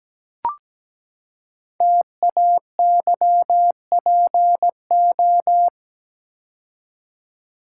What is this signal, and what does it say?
Morse code, heard as a keyed tone, about 17 wpm, 700 Hz. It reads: TAYPO
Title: Code Groups - 5 Letters - 17wpm